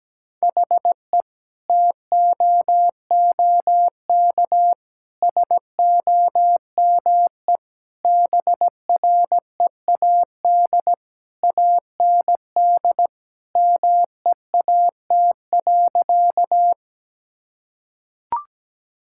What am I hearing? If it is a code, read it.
Morse code, heard as a keyed tone, about 17 wpm, 700 Hz. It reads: HE TOOK SOME BREAD AND MEAT.
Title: Sentences from Top 1600 Words 17wpm